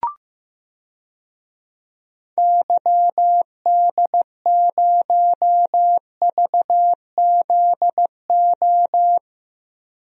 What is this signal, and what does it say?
Morse code, heard as a keyed tone, about 15 wpm, 700 Hz. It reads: YD0VZO
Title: DXCC-100 Call Signs - 6 Characters - 15wpm